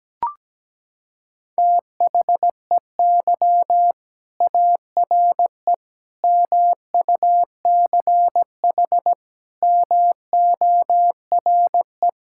Morse code, heard as a keyed tone, about 17 wpm, 700 Hz. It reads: THEY ARE MUCH MORE
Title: Sentences from Top 1000 Words - Review Part 5 - 17wpm